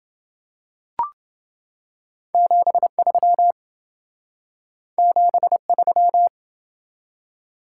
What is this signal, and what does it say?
Morse code, heard as a keyed tone, about 27 wpm, 700 Hz. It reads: 73 73